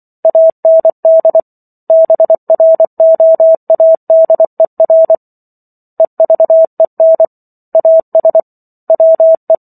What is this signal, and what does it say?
Morse code, heard as a keyed tone, about 24 wpm, 650 Hz. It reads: AND BROADER EVEN AS WE
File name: 24 WPM